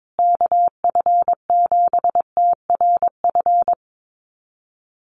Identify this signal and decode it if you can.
Morse code, heard as a keyed tone, about 22 wpm, 700 Hz. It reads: KF7TRF
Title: USA Call Signs Repeated 2x 22wpm